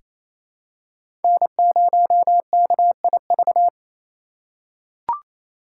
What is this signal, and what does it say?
Morse code, heard as a keyed tone, about 28 wpm, 700 Hz. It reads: N0KIV
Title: USA Call Signs - 5 Characters - 2x Character Spacing - 28wpm